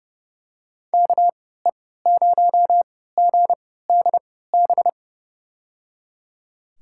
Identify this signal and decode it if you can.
Morse code, heard as a keyed tone, about 30 wpm, spaced as for 17 wpm, 700 Hz. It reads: KE0GDB
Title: USA Call Signs - 6 Characters - 3x Character Spacing - 30wpm